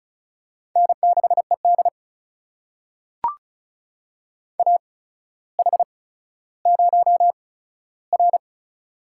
Morse code, heard as a keyed tone, about 36 wpm, spaced as for 15 wpm, 700 Hz. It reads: N6ED AH0R
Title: USA Call Signs - 4 Characters - 8x Character Spacing - 35wpm